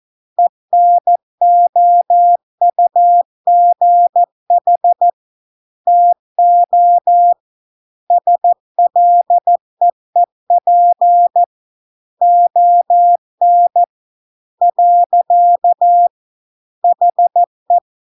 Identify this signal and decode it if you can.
Morse code, heard as a keyed tone, about 14 wpm, 700 Hz. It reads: ENOUGH TO SLEEP ON . HE